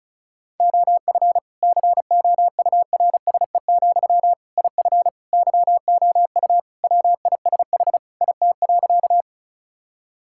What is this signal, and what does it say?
Morse code, heard as a keyed tone, about 36 wpm, 700 Hz. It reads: OF COURSE, IF YOU WISH IT.
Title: Sentences from Top 1600 Words 35wpm